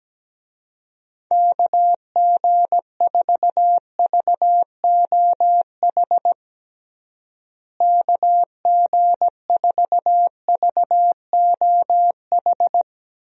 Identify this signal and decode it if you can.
Morse code, heard as a keyed tone, about 17 wpm, 700 Hz. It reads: KG4VOH KG4VOH
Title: USA Call Signs - 6 Characters - Repeated 3x - 17wpm